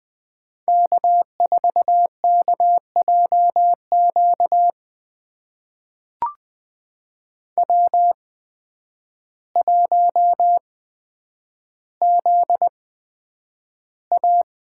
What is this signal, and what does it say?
Morse code, heard as a keyed tone, about 20 wpm, spaced as for 9 wpm, 700 Hz. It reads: K4KJQ W1ZA
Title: USA Call Signs - 8x Character Spacing - 20wpm